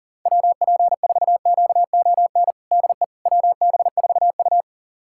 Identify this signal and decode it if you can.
Morse code, heard as a keyed tone, about 37 wpm, 700 Hz. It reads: WP4QON DE WB4U
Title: USA Callsign DE Callsign 40wpm